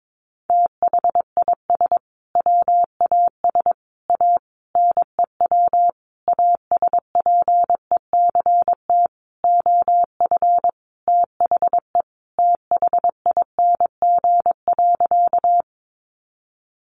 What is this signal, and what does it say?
Morse code, heard as a keyed tone, about 22 wpm, 700 Hz. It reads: THIS WAS A NEW ASPECT OF THE THING.
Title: Sentences from Top 1300 Words 22wpm